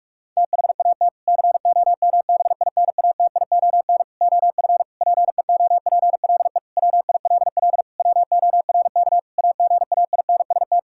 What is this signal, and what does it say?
Morse code, heard as a keyed tone, about 48 wpm, 700 Hz. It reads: THAT COMBINATION OF PEOPLE WILL WORK AGAINST
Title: Sentences from Top 1000 Words - Review Part 6 - 45wpm